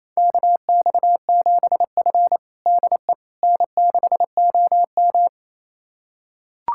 Morse code, heard as a keyed tone, about 27 wpm, 700 Hz. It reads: KX7F DE N6OM
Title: USA Callsign DE Callsign - 4 Characters - 28wpm